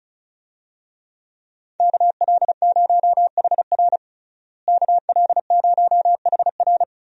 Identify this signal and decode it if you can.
Morse code, heard as a keyed tone, about 36 wpm, 700 Hz. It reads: KL0HR KL0HR
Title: USA Call Signs - 5 Characters - Repeated 3x - 35wpm